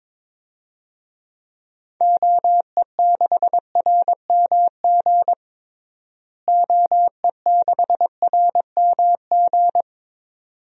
Morse code, heard as a keyed tone, about 21 wpm, 700 Hz. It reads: OE6RMG OE6RMG
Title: DXCC-100 Call Signs - 6 Characters - Repeated 2x - 22wpm